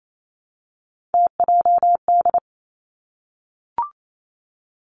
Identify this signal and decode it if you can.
Morse code, heard as a keyed tone, about 28 wpm, 700 Hz. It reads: TJD
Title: Three Challenging Letter-Number Combinations 28wpm